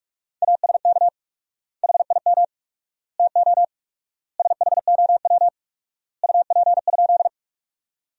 Morse code, heard as a keyed tone, about 46 wpm, 700 Hz. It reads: ASK HIM TO SHOW UP?
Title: 4x Word-Spacing - Sentences from Top 200 Words - 45wpm